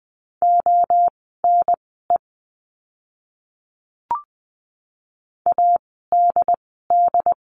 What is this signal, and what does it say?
Morse code, heard as a keyed tone, about 20 wpm, spaced as for 14 wpm, 700 Hz. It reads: ONE ADD
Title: 075 Lesson 13f - Letter L - Words 20wpm